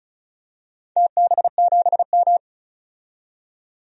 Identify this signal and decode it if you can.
Morse code, heard as a keyed tone, about 36 wpm, 700 Hz. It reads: TB7M
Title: DXCC-100 Call Signs - 4 Characters - 35wpm